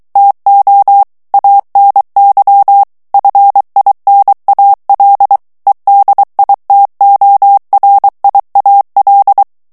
Morse code, heard as a keyed tone, about 23 wpm, 800 Hz. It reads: TO ANY FINAL EDITORIAL
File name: ANGLAIS texte clair 3 vitesse 22 mots minute